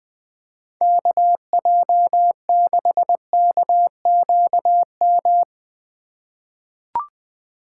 Morse code, heard as a keyed tone, about 20 wpm, 700 Hz. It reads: KJ6KQM
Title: USA Call Signs - 6 Characters - 3x Character Spacing - 20wpm